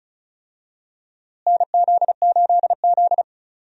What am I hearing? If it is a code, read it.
Morse code, heard as a keyed tone, about 36 wpm, 700 Hz. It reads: NZ8Z
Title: USA Call Signs - 4 Characters - Repeated 2x - 35wpm